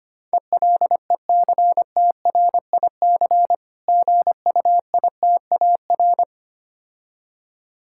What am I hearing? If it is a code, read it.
Morse code, heard as a keyed tone, about 26 wpm, 700 Hz. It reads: ELECTRIC GUITAR
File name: practice / 25wpm / Difficult Words - Speed-Racing with Hints - 25wpm